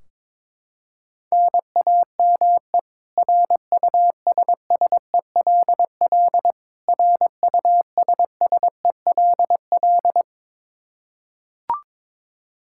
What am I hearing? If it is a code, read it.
Morse code, heard as a keyed tone, about 22 wpm, 700 Hz. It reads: NAME RUSSELL RUSSELL
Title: K1USN SST QSO Practice 22wpm